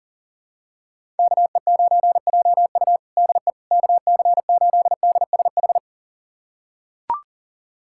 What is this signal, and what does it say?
Morse code, heard as a keyed tone, about 40 wpm, 700 Hz. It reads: KE9JU DE KC8DSH
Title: USA Callsign DE Callsign 40wpm